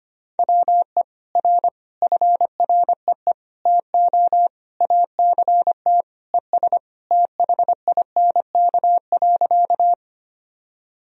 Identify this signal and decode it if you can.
Morse code, heard as a keyed tone, about 25 wpm, 700 Hz. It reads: WE R FREE TO ACT ES THINK.
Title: Sentences from Top 500 Words and Abbreviations 25wpm